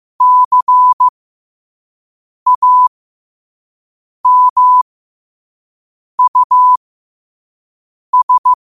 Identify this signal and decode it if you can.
Morse code, heard as a keyed tone, about 15 wpm, spaced as for 5 wpm, 1000 Hz. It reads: CAMUS
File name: qotdfiles / 5wpm